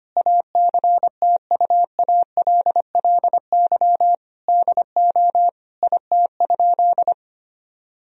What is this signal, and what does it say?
Morse code, heard as a keyed tone, about 25 wpm, 700 Hz. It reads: ACTUALLY DO IT?
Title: Sentences from Top 1000 Words - Review Part 4 - 25wpm